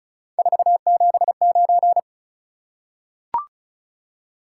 Morse code, heard as a keyed tone, about 36 wpm, 700 Hz. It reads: 479
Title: Top 180 QSO Elements 35wpm